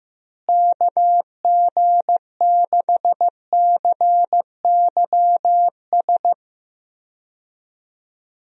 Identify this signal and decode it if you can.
Morse code, heard as a keyed tone, about 14 wpm, 700 Hz. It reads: KG6CYS
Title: USA Call Signs - 6 Characters - Repeated 2x - 15wpm